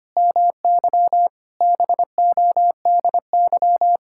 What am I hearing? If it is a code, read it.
Morse code, heard as a keyed tone, about 25 wpm, 700 Hz. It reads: MY BODY
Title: Sentences from Top 1000 Words - Review Part 6 - 25wpm